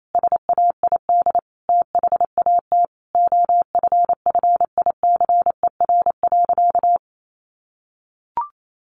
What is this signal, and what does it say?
Morse code, heard as a keyed tone, about 28 wpm, 700 Hz. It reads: SAID THAT OFFICER.
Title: Sentences from Top 1200 Words 28wpm